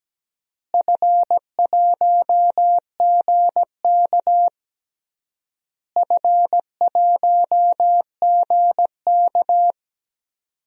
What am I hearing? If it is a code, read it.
Morse code, heard as a keyed tone, about 17 wpm, 700 Hz. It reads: F1GK F1GK
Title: DXCC-100 Call Signs - 4 Characters - Repeated 3x - 17wpm